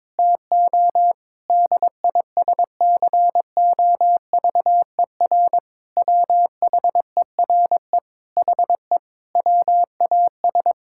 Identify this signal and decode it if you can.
Morse code, heard as a keyed tone, about 22 wpm, 700 Hz. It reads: TO DISCOVER WHERE HE WAS
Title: Sentences from Top 1600 Words 22wpm